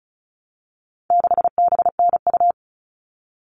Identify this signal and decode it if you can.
Morse code, heard as a keyed tone, about 34 wpm, 700 Hz. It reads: 6BNU
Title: All Cut Numbers - 4-Digit 35wpm